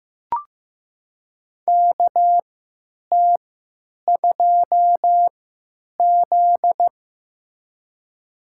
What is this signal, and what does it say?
Morse code, heard as a keyed tone, about 15 wpm, spaced as for 9 wpm, 700 Hz. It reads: KT2Z
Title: USA Call Signs - 3x Character Spacing - 15wpm